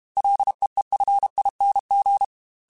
Morse code, heard as a keyed tone, about 32 wpm, 800 Hz. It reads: LEEFING
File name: Ls / Single L m3s / Single L Prefixe 3x2 Twice @30